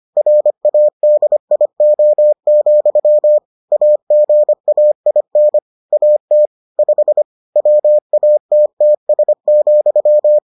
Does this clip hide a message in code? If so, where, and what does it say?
Yes: Morse code from 0.2 to 10.4 s, heard as a keyed tone, about 25 wpm, 600 Hz. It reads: RADIO, AGAIN AT 5 WATTS,